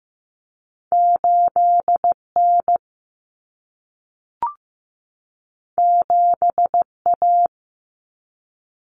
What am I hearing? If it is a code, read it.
Morse code, heard as a keyed tone, about 15 wpm, 700 Hz. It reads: 8N 7A
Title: Common Cut Numbers - 2-Digit 15wpm